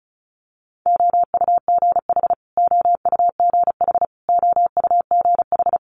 Morse code, heard as a keyed tone, about 36 wpm, 700 Hz. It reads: OUGH OUGH OUGH
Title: Multi-letter Phonemes Repeated 3x 35wpm